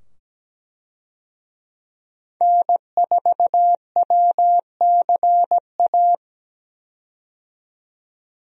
Morse code, heard as a keyed tone, about 17 wpm, 700 Hz. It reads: N4WCA
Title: USA Call Signs - 5 Characters - 17wpm